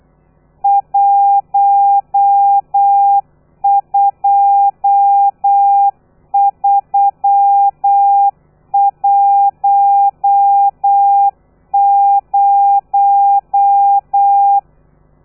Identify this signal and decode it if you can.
Morse code, heard as a keyed tone, about 8 wpm, 800 Hz. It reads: 12310
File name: telegrafia / i2tfj / Lez13 35